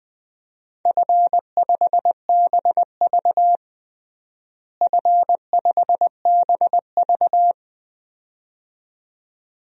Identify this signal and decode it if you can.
Morse code, heard as a keyed tone, about 20 wpm, 700 Hz. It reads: F5BV F5BV
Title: DXCC-100 Call Signs - 4 Characters - Repeated 2x - 20wpm